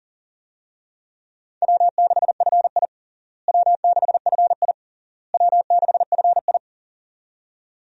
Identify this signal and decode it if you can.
Morse code, heard as a keyed tone, about 40 wpm, 700 Hz. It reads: W6FI W6FI W6FI
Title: USA Call Signs - 4 Characters - Repeated 3x - 40wpm